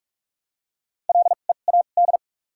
Morse code, heard as a keyed tone, about 46 wpm, spaced as for 33 wpm, 700 Hz. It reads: LEAD
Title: 2x Character Spacing - Top 500 Words - 45wpm